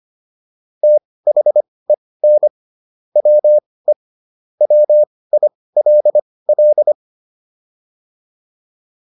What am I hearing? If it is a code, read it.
Morse code, heard as a keyed tone, about 25 wpm, spaced as for 18 wpm, 600 Hz. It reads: THEN WE WILL